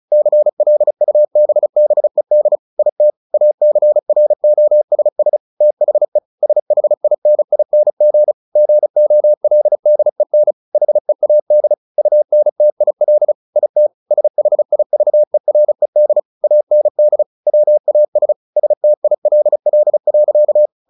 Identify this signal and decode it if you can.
Morse code, heard as a keyed tone, about 34 wpm, 600 Hz. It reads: CLUBBED IT ACROSS THE SHINING GOLDEN HEAD UNTIL IT SHIVERED AND WAS STILL.
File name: the old man and the sea 2 35wpm